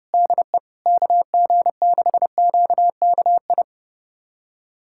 Morse code, heard as a keyed tone, about 30 wpm, 700 Hz. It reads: DE KG6QKI
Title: USA Callsign DE Callsign 30wpm